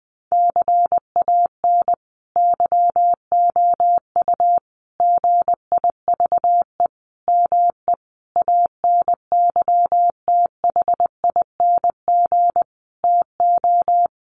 Morse code, heard as a keyed tone, about 20 wpm, 700 Hz. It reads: CAN YOU GIVE ME ANYTHING TO
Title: Sentences from Top 1500 Words 20wpm